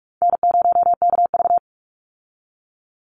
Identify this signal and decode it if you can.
Morse code, heard as a keyed tone, about 44 wpm, 700 Hz. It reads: N0KV
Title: Call Signs - 45wpm